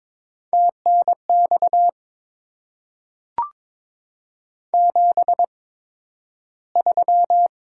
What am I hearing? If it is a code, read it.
Morse code, heard as a keyed tone, about 22 wpm, spaced as for 12 wpm, 700 Hz. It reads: TNX 73
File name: Top 20 QSO Elements - 8x Character Spacing - 22wpm